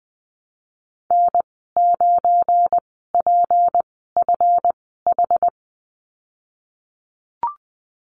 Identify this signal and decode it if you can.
Morse code, heard as a keyed tone, about 19 wpm, spaced as for 14 wpm, 700 Hz. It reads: N9PFH
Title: 245 Lesson 39g - Letter z - Call Signs 20wpm